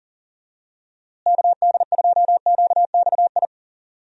Morse code, heard as a keyed tone, about 38 wpm, 700 Hz. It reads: KD2QXI
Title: USA Call Signs - 6 Characters - Repeated 3x - 40wpm